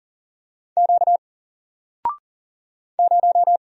Morse code, heard as a keyed tone, about 40 wpm, 700 Hz. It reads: Q 0
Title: Single Letter-Number Alt 40wpm